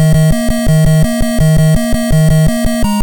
There is no code.